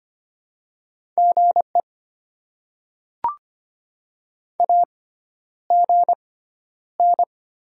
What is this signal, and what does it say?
Morse code, heard as a keyed tone, about 25 wpm, spaced as for 11 wpm, 700 Hz. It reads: GE AGN
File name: Top 20 QSO Elements - 6x Character Spacing - 25wpm